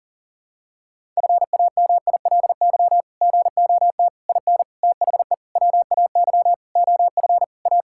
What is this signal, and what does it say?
Morse code, heard as a keyed tone, about 40 wpm, 700 Hz. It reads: FAMILY GOT IN THE WAY OF A
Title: Sentences from Top 1000 Words - Review Part 6 - 40wpm